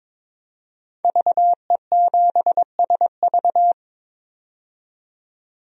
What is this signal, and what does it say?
Morse code, heard as a keyed tone, about 21 wpm, 700 Hz. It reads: VE7SV
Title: DXCC-100 Call Signs - 5 Characters - 22wpm